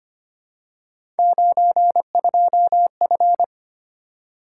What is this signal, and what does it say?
Morse code, heard as a keyed tone, about 25 wpm, 700 Hz. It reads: 92F
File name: Three Letter-Number Combinations - 25wpm